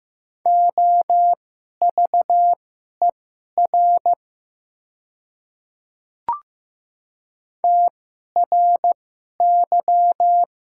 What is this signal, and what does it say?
Morse code, heard as a keyed tone, about 15 wpm, spaced as for 11 wpm, 700 Hz. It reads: OVER TRY
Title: 2x Character Spacing - Top 200 Words - 15wpm